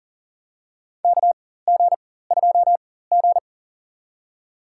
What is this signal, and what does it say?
Morse code, heard as a keyed tone, about 40 wpm, spaced as for 19 wpm, 700 Hz. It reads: KG2G
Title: USA Call Signs - 4x Character Spacing - 40wpm